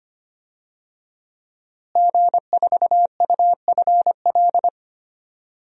Morse code, heard as a keyed tone, about 25 wpm, 700 Hz. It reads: G4UFL
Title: DXCC-100 Call Signs - 5 Characters - Repeated 3x - 25wpm